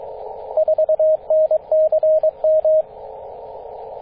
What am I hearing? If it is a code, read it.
Morse code, heard as a keyed tone, about 22 wpm, 630 Hz, with radio noise. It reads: 4NCM
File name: 9m0o-24-891mhz-cw